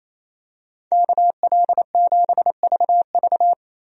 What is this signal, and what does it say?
Morse code, heard as a keyed tone, about 28 wpm, 700 Hz. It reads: KL7VV
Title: USA Call Signs - 5 Characters - 28wpm